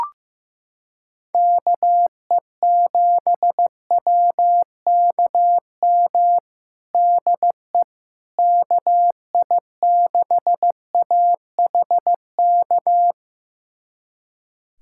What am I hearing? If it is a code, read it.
Morse code, heard as a keyed tone, about 16 wpm, 700 Hz. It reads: KE7WKM DE KI6AHK
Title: USA Callsign DE Callsign - 6 Characters - 15wpm